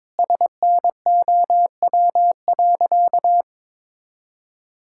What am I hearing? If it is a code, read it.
Morse code, heard as a keyed tone, about 22 wpm, 700 Hz. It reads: SNOW.
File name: practice / 22wpm / Sentences with 8-Word Phrases - 22wpm